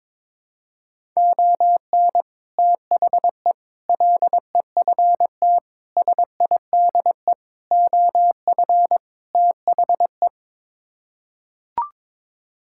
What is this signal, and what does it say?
Morse code, heard as a keyed tone, about 22 wpm, 700 Hz. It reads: ON THE LEFT SIDE OF THE
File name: practice / 22wpm / Sets of 6 Words - 22wpm